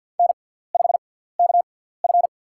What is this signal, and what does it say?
Morse code, heard as a keyed tone, about 53 wpm, spaced as for 17 wpm, 700 Hz. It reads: N5KF
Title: USA Call Signs - 4 Characters - 6x Character Spacing - 50wpm